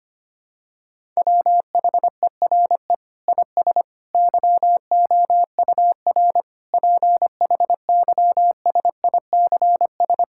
Morse code, heard as a keyed tone, about 25 wpm, 700 Hz. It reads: WHERE IS YOUR PHYSICS
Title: Sentences from Top 1000 Words - Review Part 5 - 25wpm